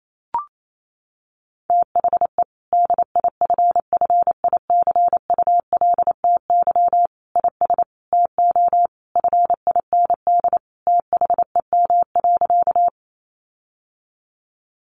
Morse code, heard as a keyed tone, about 28 wpm, 700 Hz. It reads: THE DIFFICULTY IS TO FIND THEM.
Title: Sentences from Top 1500 Words 28wpm